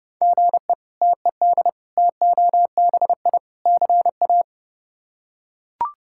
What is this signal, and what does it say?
Morse code, heard as a keyed tone, about 31 wpm, 700 Hz. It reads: GE TED TOBI CA
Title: K1USN SST QSO Practice - 30wpm